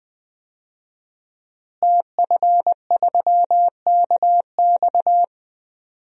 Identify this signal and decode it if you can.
Morse code, heard as a keyed tone, about 20 wpm, 700 Hz. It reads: TF3KX
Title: DXCC-100 Call Signs - 5 Characters - Repeated 2x - 20wpm